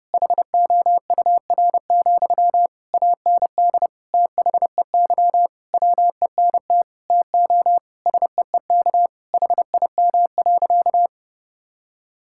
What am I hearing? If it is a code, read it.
Morse code, heard as a keyed tone, about 30 wpm, 700 Hz. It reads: HOUR, AND THEY WENT TO SEEK HIM.
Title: Sentences from Top 1600 Words 30wpm